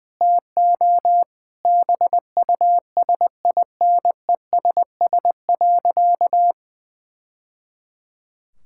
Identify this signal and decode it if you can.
Morse code, heard as a keyed tone, about 20 wpm, 700 Hz. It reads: TO BUSINESS.
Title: Sentences from Top 1000 Words - Review Part 6 - 20wpm